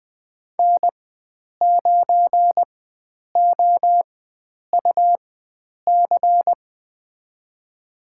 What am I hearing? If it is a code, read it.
Morse code, heard as a keyed tone, about 20 wpm, spaced as for 9 wpm, 700 Hz. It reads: N9OUC
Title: USA Call Signs - 5 Characters - 4x Character Spacing - 20wpm